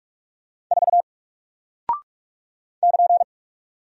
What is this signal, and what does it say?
Morse code, heard as a keyed tone, about 44 wpm, 700 Hz. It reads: 4 <KN>
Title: Single Character (Letter:Number:Punctuation:Prosign) Alt 45wpm